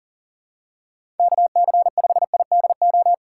Morse code, heard as a keyed tone, about 40 wpm, 700 Hz. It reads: KC5IDO
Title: USA Call Signs - 6 Characters - 6x Character Spacing - 40wpm